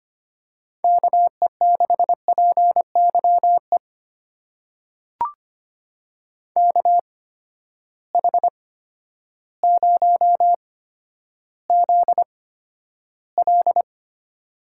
Morse code, heard as a keyed tone, about 25 wpm, spaced as for 11 wpm, 700 Hz. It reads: KE6PYE KH0ZL
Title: USA Call Signs - 8x Character Spacing - 25wpm